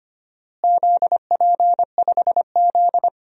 Morse code, heard as a keyed tone, about 25 wpm, 700 Hz. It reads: ZP5Z
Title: DXCC-100 Call Signs - 4 Characters - Repeated 3x - 25wpm